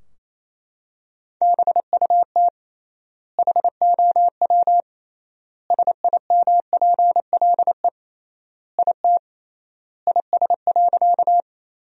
Morse code, heard as a keyed tone, about 28 wpm, 700 Hz. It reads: BUT HOW SIMPLE IT IS.
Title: 3x Word-Spacing - Sentences from Top 400 Words - 28wpm